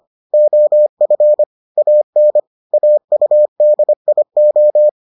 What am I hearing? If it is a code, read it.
Morse code, heard as a keyed tone, about 25 wpm, 600 Hz. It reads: OF AN AUDIO